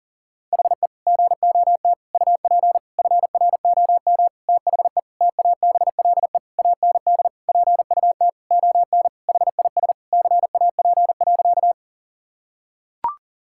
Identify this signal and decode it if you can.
Morse code, heard as a keyed tone, about 40 wpm, 700 Hz. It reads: HE GOT UP FROM THE TABLE AND PUT ON HIS CAP.
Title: Sentences from Top 1600 Words 40wpm